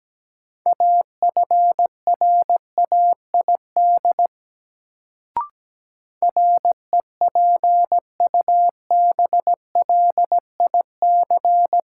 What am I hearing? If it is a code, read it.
Morse code, heard as a keyed tone, about 17 wpm, 700 Hz. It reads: AFRAID REPUBLIC
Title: Sentences from Top 1500 Words - New Word Drill - 17wpm